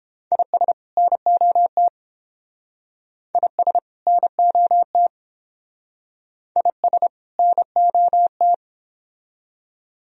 Morse code, heard as a keyed tone, about 29 wpm, 700 Hz. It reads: IS NOT IS NOT IS NOT